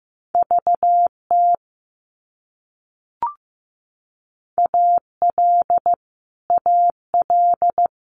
Morse code, heard as a keyed tone, about 15 wpm, 700 Hz. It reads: VT AL AL
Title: US States Repeated 3x 15wpm